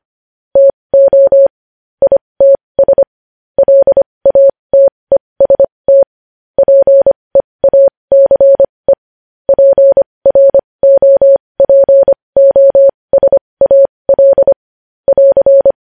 Morse code, heard as a keyed tone, about 25 wpm, spaced as for 20 wpm, 550 Hz. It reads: TO ITS LATEST PEACE PROPOSAL <AR>
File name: podcast / News headlines-1x-25x20